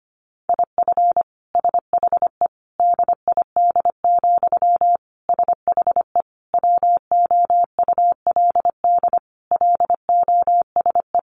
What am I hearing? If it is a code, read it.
Morse code, heard as a keyed tone, about 25 wpm, 700 Hz. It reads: IF SHE DID, SHE WOULD LOSE
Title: Sentences from Top 1400 Words 25wpm